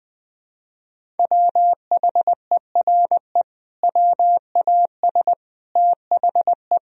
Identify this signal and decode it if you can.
Morse code, heard as a keyed tone, about 20 wpm, 700 Hz. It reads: WHERE WAS THE
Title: Sentences from Top 1000 Words - Review Part 3 - 20wpm